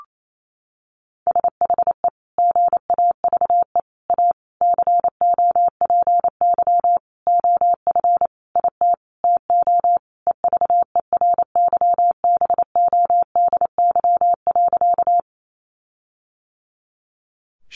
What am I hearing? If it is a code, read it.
Morse code, heard as a keyed tone, about 28 wpm, 700 Hz. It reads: SHE GAVE A COPY OF IT TO EVERYBODY.
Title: Sentences from Top 1400 Words 28wpm